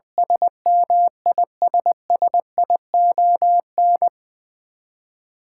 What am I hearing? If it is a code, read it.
Morse code, heard as a keyed tone, about 20 wpm, 700 Hz. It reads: SMISSION
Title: Twelve Letter Words - 20wpm